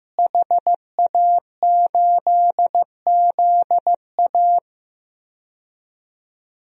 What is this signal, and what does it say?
Morse code, heard as a keyed tone, about 15 wpm, 700 Hz. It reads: HA8ZA
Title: DXCC-100 Call Signs - 5 Characters - Repeated 3x - 15wpm